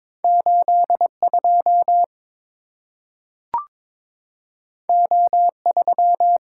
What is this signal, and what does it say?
Morse code, heard as a keyed tone, about 22 wpm, 700 Hz. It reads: 82 O3
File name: practice / 22wpm / Two Letter-Number Combinations - 22wpm